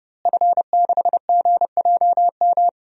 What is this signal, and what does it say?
Morse code, heard as a keyed tone, about 32 wpm, 700 Hz. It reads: F6GJM
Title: DXCC-100 Call Signs - 5 Characters - Repeated 3x - 30wpm